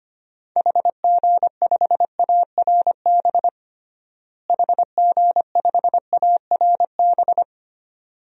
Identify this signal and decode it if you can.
Morse code, heard as a keyed tone, about 25 wpm, 700 Hz. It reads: HG5ARB HG5ARB
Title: DXCC-100 Call Signs - 6 Characters - Repeated 2x - 25wpm